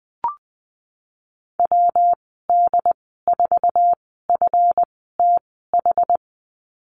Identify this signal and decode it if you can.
Morse code, heard as a keyed tone, about 20 wpm, spaced as for 15 wpm, 700 Hz. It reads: WD4FTH
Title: 154 Lesson 25g - Letter p - Call Signs 20wpm